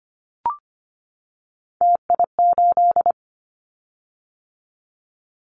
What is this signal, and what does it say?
Morse code, heard as a keyed tone, about 25 wpm, 700 Hz. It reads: TI8
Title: Callsign Prefixes for Top 100 DXCC - 25wpm